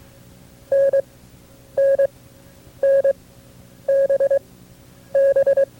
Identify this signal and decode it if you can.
Morse code, heard as a keyed tone, about 22 wpm, spaced as for 9 wpm, 550 Hz. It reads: NNNBB
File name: FreeFiles / CW-practice / KB6MT-tape2-side2